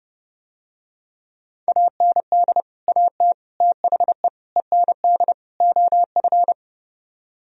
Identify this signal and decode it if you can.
Morse code, heard as a keyed tone, about 30 wpm, 700 Hz. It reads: AND AT THE END OF